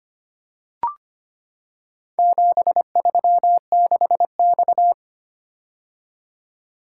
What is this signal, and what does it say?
Morse code, heard as a keyed tone, about 25 wpm, 700 Hz. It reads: 736X
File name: practice / 25wpm / Four Letter-Number Combinations - 25wpm